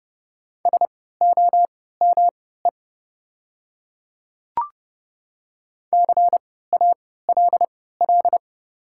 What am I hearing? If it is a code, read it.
Morse code, heard as a keyed tone, about 30 wpm, spaced as for 17 wpm, 700 Hz. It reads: SOME CALL
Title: 3x Character Spacing - Top 200 Words - 30wpm